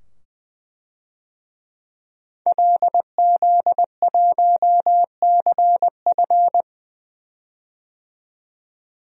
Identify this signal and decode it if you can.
Morse code, heard as a keyed tone, about 20 wpm, 700 Hz. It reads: LZ1CF
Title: DXCC-100 Call Signs - 5 Characters - 20wpm